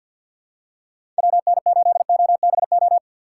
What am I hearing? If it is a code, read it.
Morse code, heard as a keyed tone, about 50 wpm, 700 Hz. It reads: WN8OBO
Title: USA Call Signs - 6 Characters - 3x Character Spacing - 50wpm